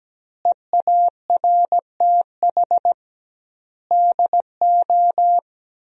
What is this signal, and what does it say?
Morse code, heard as a keyed tone, about 17 wpm, 700 Hz. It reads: EARTH DO
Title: 2x Word-Spacing - Sentences from Top 200 Words - 17wpm